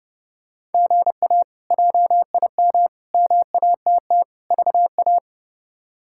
Morse code, heard as a keyed tone, about 28 wpm, 700 Hz. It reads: GA JIM MATT VA
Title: K1USN SST QSO Practice - 30wpm